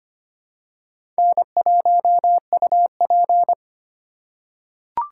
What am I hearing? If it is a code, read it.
Morse code, heard as a keyed tone, about 25 wpm, 700 Hz. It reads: N1UP
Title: USA Call Signs - 4 Characters - 4x Character Spacing - 25wpm